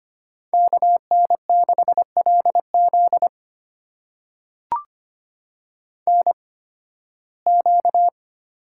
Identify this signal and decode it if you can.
Morse code, heard as a keyed tone, about 25 wpm, spaced as for 16 wpm, 700 Hz. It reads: KN6LZ NQ
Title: USA Call Signs - 8x Character Spacing - 25wpm